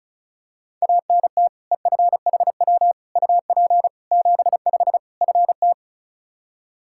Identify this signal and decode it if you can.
Morse code, heard as a keyed tone, about 36 wpm, 700 Hz. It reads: ANT EFHW UP 75 FT
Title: Practice QSO - Exchange 2 - Info - 35wpm